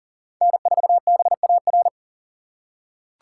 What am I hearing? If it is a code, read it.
Morse code, heard as a keyed tone, about 40 wpm, 700 Hz. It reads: N4BAR
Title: USA Call Signs - 5 Characters - 6x Character Spacing - 40wpm